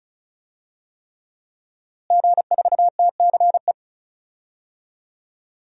Morse code, heard as a keyed tone, about 34 wpm, 700 Hz. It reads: G4TCE
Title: DXCC-100 Call Signs - 5 Characters - Repeated 3x - 35wpm